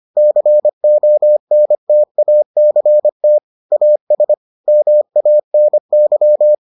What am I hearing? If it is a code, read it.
Morse code, heard as a keyed tone, about 25 wpm, 600 Hz. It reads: CONTACT AS MANY